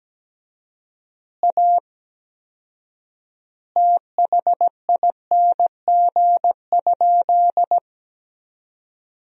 Morse code, heard as a keyed tone, about 17 wpm, 700 Hz. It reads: A THING?
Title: 4x Word-Spacing - Sentences from Top 200 Words - 17wpm